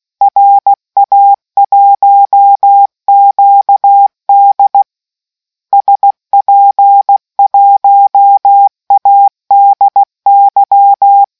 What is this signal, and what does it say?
Morse code, heard as a keyed tone, about 16 wpm, 800 Hz. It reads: RA1QD SP1ADY